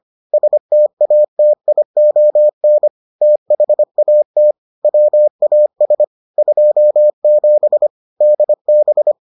Morse code, heard as a keyed tone, about 25 wpm, 600 Hz. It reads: STATION THAT WAS 27 DB